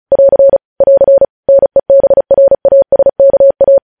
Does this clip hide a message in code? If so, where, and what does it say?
Yes: Morse code from 0.1 to 3.8 s, heard as a keyed tone, about 36 wpm, 550 Hz. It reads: <AR> <AR> NEBRASKA